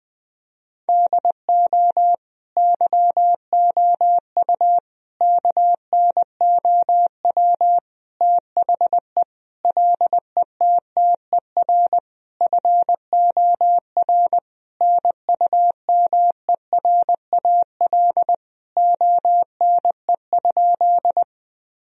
Morse code, heard as a keyed tone, about 20 wpm, 700 Hz. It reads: DO YOU KNOW THE LETTER FOR NUMERAL ONE?
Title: Sentences from Top 1000 Words - Review Part 2 - 20wpm